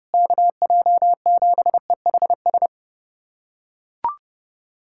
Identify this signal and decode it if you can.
Morse code, heard as a keyed tone, about 30 wpm, 700 Hz. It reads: KJ7EHS
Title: USA Callsign DE Callsign 30wpm